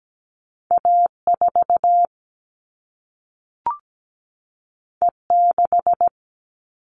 Morse code, heard as a keyed tone, about 17 wpm, 700 Hz. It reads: A4 E6
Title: All Cut Numbers - 2-Digit 17wpm